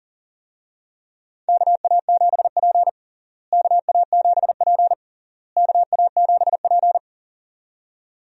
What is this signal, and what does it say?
Morse code, heard as a keyed tone, about 40 wpm, 700 Hz. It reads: KA7P KA7P KA7P
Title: USA Call Signs - 4 Characters - Repeated 3x - 40wpm